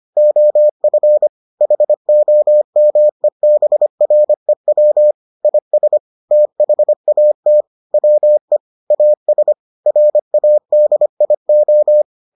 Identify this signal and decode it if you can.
Morse code, heard as a keyed tone, about 25 wpm, 600 Hz. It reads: OF HOMEBREW IS THAT WE AS RADIO